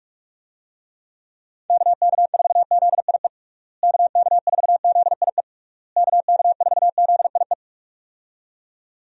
Morse code, heard as a keyed tone, about 41 wpm, 700 Hz. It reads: KK4ZIE KK4ZIE KK4ZIE
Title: USA Call Signs - 6 Characters - Repeated 3x - 45wpm